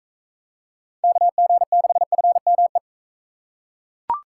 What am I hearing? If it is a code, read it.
Morse code, heard as a keyed tone, about 40 wpm, 700 Hz. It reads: KG6FME